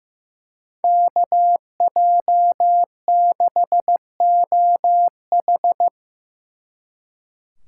Morse code, heard as a keyed tone, about 17 wpm, 700 Hz. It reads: KJ6OH
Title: Speed Racing - USA 2x2 Call Signs 15wpm